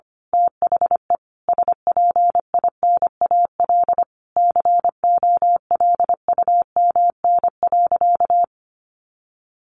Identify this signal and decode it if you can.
Morse code, heard as a keyed tone, about 25 wpm, 700 Hz. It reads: THE SPINAL COLUMN.
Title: A Sound Of Thunder 25wpm